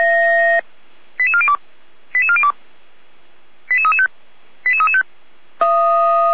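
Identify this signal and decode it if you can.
Five-tone selective call (ZVEI), heard as quick runs of five tones, tones of about 70 ms. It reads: 88402 88402 88335 88335
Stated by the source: Five-tone ZVEI selective call 89350 received on an SDR.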